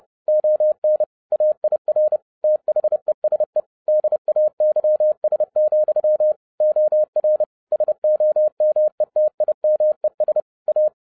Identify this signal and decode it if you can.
Morse code, heard as a keyed tone, about 30 wpm, 610 Hz. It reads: ON AIR THESE DAYS, OR SOMETIMES A
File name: 2011-07-mp3 / n20 GREETINGS FROM SINGAPORE 30wpm